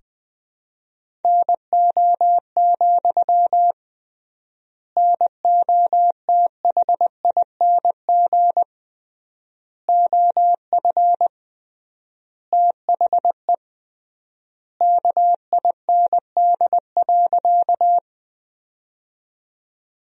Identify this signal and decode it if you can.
Morse code, heard as a keyed tone, about 21 wpm, 700 Hz. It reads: NO, NOTHING OF THE KIND.
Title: 3x Word-Spacing - Sentences from Top 500 Words - 20wpm